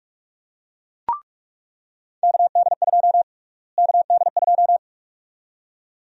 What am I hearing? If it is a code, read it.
Morse code, heard as a keyed tone, about 41 wpm, 700 Hz. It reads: KD2 KD2
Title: USA Call Sign Prefixes - Repeated 2x - 45wpm